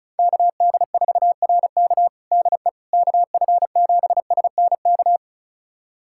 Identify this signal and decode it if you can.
Morse code, heard as a keyed tone, about 34 wpm, 700 Hz. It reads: KD4RK DE KF7SNK
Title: USA Callsign DE Callsign 35wpm